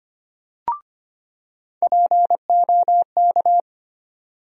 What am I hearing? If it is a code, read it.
Morse code, heard as a keyed tone, about 25 wpm, 700 Hz. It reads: POK
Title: USA Call Sign Postfixes - 3 Letters 25wpm